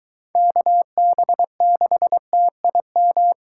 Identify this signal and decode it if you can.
Morse code, heard as a keyed tone, about 23 wpm, 700 Hz. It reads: KB6TIM